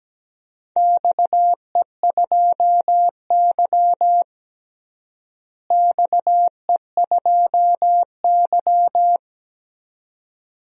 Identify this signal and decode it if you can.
Morse code, heard as a keyed tone, about 17 wpm, 700 Hz. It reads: XE2Y XE2Y
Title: DXCC-100 Call Signs - 4 Characters - Repeated 2x - 17wpm